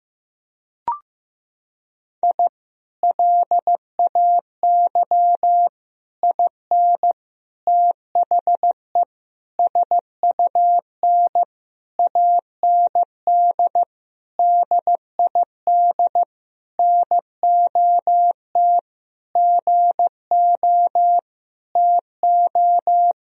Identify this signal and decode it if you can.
Morse code, heard as a keyed tone, about 15 wpm, 700 Hz. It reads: I LAY IN THE SUN AND DID NOT GO TO
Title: Sentences from Top 1000 Words - Review Part 2 - 15wpm